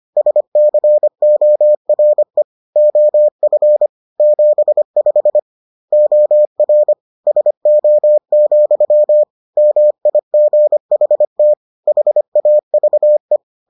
Morse code, heard as a keyed tone, about 25 wpm, 600 Hz. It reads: SCORE OF 75 OR SO, MIGHT HAVE